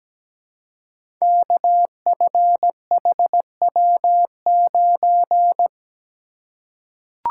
Morse code, heard as a keyed tone, about 17 wpm, 700 Hz. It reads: KFHW9
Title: Code Groups - 5 Letter:Numbers - 17wpm